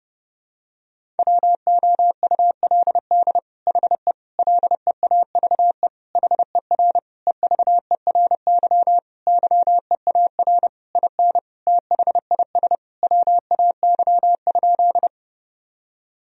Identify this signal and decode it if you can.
Morse code, heard as a keyed tone, about 32 wpm, 700 Hz. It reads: WOULD HE LEAVE HER EVERY YEAR IN THIS WAY?
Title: Sentences from Top 1000 Words - Review Part 1 - 30wpm